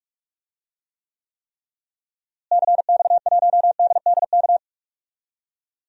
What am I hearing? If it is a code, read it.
Morse code, heard as a keyed tone, about 45 wpm, 700 Hz. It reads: CX1DDK
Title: DXCC-100 Call Signs - 6 Characters - Repeated 3x - 45wpm